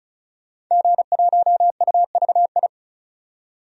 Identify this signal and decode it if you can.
Morse code, heard as a keyed tone, about 36 wpm, 700 Hz. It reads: G1UVI
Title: DXCC-100 Call Signs - 5 Characters - 35wpm